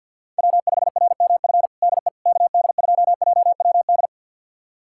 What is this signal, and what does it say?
Morse code, heard as a keyed tone, about 47 wpm, 700 Hz. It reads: W5RMF DE KD2JWD
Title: USA Callsign DE Callsign 50wpm